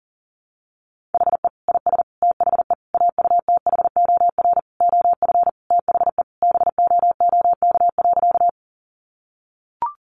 Morse code, heard as a keyed tone, about 40 wpm, 700 Hz. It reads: HE IS THE AUTHOR OF THE BOOK.
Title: Sentences from Top 1200 Words 40wpm